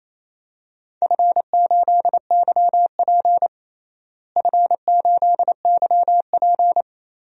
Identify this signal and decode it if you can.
Morse code, heard as a keyed tone, about 28 wpm, 700 Hz. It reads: F8YP F8YP
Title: DXCC-100 Call Signs - 4 Characters - Repeated 2x - 28wpm